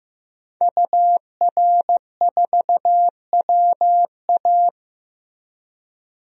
Morse code, heard as a keyed tone, about 15 wpm, 700 Hz. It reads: UR4WA
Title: DXCC-100 Call Signs - 5 Characters - 15wpm